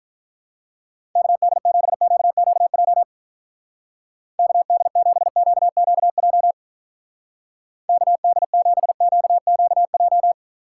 Morse code, heard as a keyed tone, about 45 wpm, 700 Hz. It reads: KD7QQJ KD7QQJ KD7QQJ